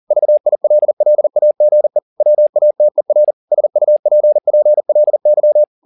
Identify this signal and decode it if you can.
Morse code, heard as a keyed tone, about 40 wpm, 600 Hz. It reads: VILLAGE WATER SUPPLY